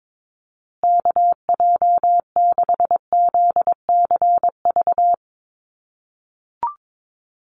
Morse code, heard as a keyed tone, about 22 wpm, 700 Hz. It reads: KJ6ZCV
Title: USA Call Signs Repeated 3x 22wpm